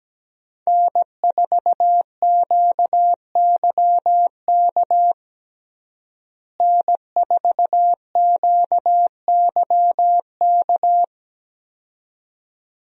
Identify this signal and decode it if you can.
Morse code, heard as a keyed tone, about 17 wpm, 700 Hz. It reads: N4QYK N4QYK
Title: USA Call Signs - 5 Characters - Repeated 2x - 17wpm